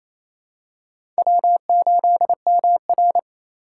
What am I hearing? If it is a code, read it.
Morse code, heard as a keyed tone, about 28 wpm, 700 Hz. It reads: W8MR
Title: USA Call Signs - 4 Characters - 2x Character Spacing - 28wpm